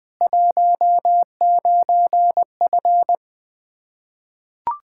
Morse code, heard as a keyed tone, about 19 wpm, 700 Hz. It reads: 19F
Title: Practice QSO - Exchange 2 - Info - 20wpm